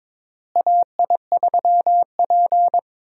Morse code, heard as a keyed tone, about 22 wpm, 700 Hz. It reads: AI3P
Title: USA Call Signs - 4 Characters - Repeated 3x - 22wpm